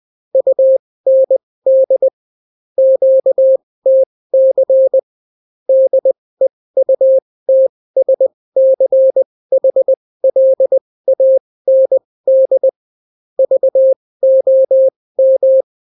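Morse code, heard as a keyed tone, about 20 wpm, spaced as for 16 wpm, 530 Hz. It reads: UND QTC DEUTSCHLAND VOM
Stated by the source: Daily Morse weather and top stories at 16 wpm, clean without QRN.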